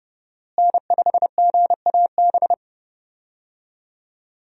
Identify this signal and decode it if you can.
Morse code, heard as a keyed tone, about 30 wpm, 700 Hz. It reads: N5GAB
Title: USA Call Signs - 5 Characters - Repeated 2x - 30wpm